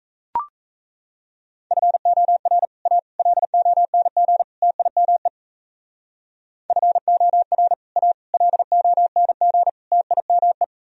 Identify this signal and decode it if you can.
Morse code, heard as a keyed tone, about 35 wpm, 700 Hz. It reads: FOR A LONG TIME FOR A LONG TIME
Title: Speed-Racing - Sets of 4 Words 28wpm